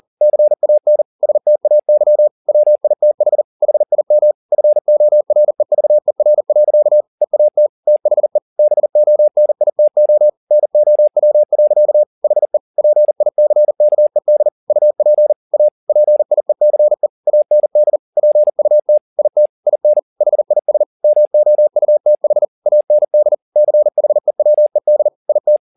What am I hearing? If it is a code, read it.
Morse code, heard as a keyed tone, about 40 wpm, 600 Hz. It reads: CAN STAY WITH HIM FOREVER. EAT THE BONITO NOW. HE PICKED UP A PIECE AND PUT IT IN HIS MOUTH AND CHEWED IT